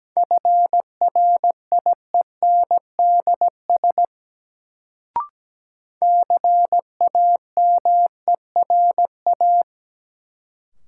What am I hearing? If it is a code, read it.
Morse code, heard as a keyed tone, about 17 wpm, 700 Hz. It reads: FRIENDS CAMERA
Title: Sentences from Top 1000 Words - Review Part 5 - New Word Drill - 17wpm